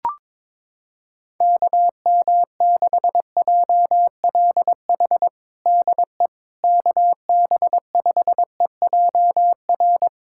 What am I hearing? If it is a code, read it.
Morse code, heard as a keyed tone, about 22 wpm, 700 Hz. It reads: KM6JLH DE KB5EJR
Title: USA Callsign DE Callsign - 6 Characters - 22wpm